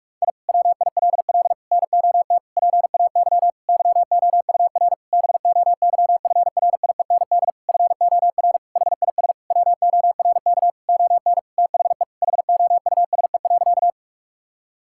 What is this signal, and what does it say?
Morse code, heard as a keyed tone, about 41 wpm, 700 Hz. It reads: I WILL NOT PAY YOUR BOYFRIEND FOR HIS WORK ON THE HOUSE.
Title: Sentences from Top 1000 Words - Review Part 5 - 45wpm